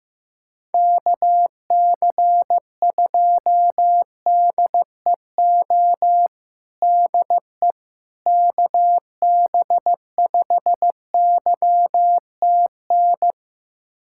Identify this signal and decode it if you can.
Morse code, heard as a keyed tone, about 15 wpm, 700 Hz. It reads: KC2DEO DE KB5YTN
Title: USA Callsign DE Callsign 15wpm